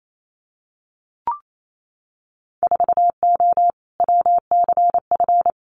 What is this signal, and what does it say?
Morse code, heard as a keyed tone, about 28 wpm, 700 Hz. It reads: 4O WCF
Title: Winter Field Day Exchanges Encore 28wpm